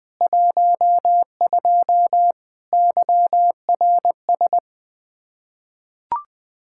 Morse code, heard as a keyed tone, about 20 wpm, 700 Hz. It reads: 12 YRS
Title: Practice QSO - Exchange 3 - More Info - 20wpm